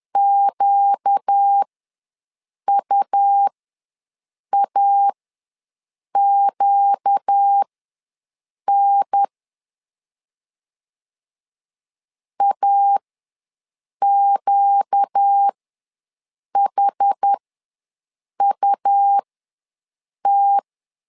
Morse code, heard as a keyed tone, about 10 wpm, spaced as for 6 wpm, 800 Hz. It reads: QUAQN AQHUT
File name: LEZIONI / 4 Q ws 10wpm 2306